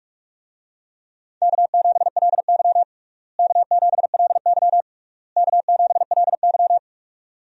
Morse code, heard as a keyed tone, about 46 wpm, 700 Hz. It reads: K7LY K7LY K7LY
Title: USA Call Signs - 4 Characters - Repeated 3x - 45wpm